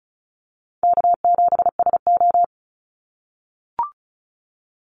Morse code, heard as a keyed tone, about 34 wpm, 700 Hz. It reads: K7SO
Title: NA SOTA Activator Callsigns 35wpm